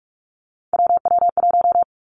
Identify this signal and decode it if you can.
Morse code, heard as a keyed tone, about 46 wpm, 700 Hz. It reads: WW1
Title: Three Challenging Letter-Number Combinations 45wpm